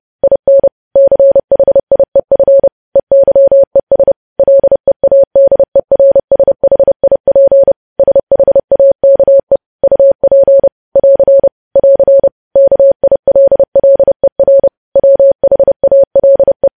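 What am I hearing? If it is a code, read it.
Morse code, heard as a keyed tone, about 30 wpm, 550 Hz. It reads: IN CHIEF EYES LEADERSHIP SHAKE UP <AR> <AR> KILLER WHALE
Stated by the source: Morse code practice of news headlines at 30 wpm, preceded by vvv.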